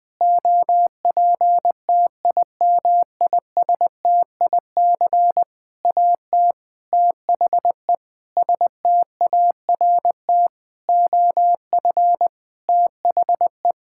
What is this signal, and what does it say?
Morse code, heard as a keyed tone, about 21 wpm, 700 Hz. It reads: OPTIMISTIC AT THE START OF THE
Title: Sentences with 6-Word Phrases - Spoken Beginning - 20wpm